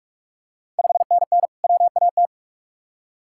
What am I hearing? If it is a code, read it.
Morse code, heard as a keyed tone, about 45 wpm, 700 Hz. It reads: 5NN WAT
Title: Ontario QSO Party Exchange 45wpm